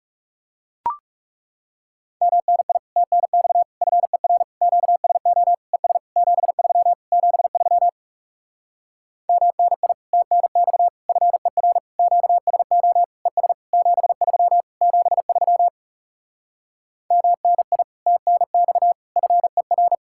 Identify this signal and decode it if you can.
Morse code, heard as a keyed tone, about 40 wpm, 700 Hz. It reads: MNI TNX FER QSO ES 73 73 MNI TNX FER QSO ES 73 73 MNI TNX FER